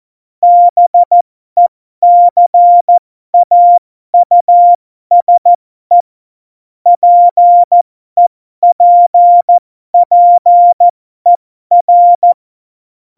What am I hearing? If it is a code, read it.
Morse code, heard as a keyed tone, about 14 wpm, spaced as for 12 wpm, 700 Hz. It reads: BECAUSE PEPPER
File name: Morsekurs / Witze